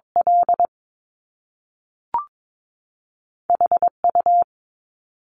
Morse code, heard as a keyed tone, about 22 wpm, 700 Hz. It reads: L HU
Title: Top Two Letter Combinations in Words 22wpm